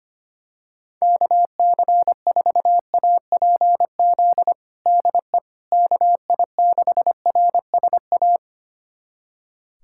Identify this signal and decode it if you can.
Morse code, heard as a keyed tone, about 25 wpm, 700 Hz. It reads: KC4APZ DE KI6RSA
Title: USA Callsign DE Callsign - 6 Characters - 25wpm